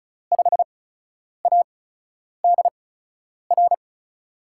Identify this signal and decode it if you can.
Morse code, heard as a keyed tone, about 34 wpm, spaced as for 10 wpm, 700 Hz. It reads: 5ADR
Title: USA Call Signs - 8x Character Spacing - 35wpm